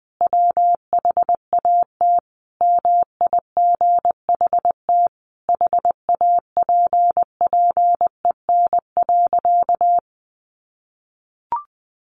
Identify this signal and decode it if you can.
Morse code, heard as a keyed tone, about 20 wpm, 700 Hz. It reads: WHAT MIGHT HAPPEN.
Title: Sentences from Top 400 Words 20wpm